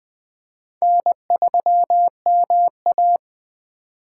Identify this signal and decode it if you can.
Morse code, heard as a keyed tone, about 20 wpm, 700 Hz. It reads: N3MA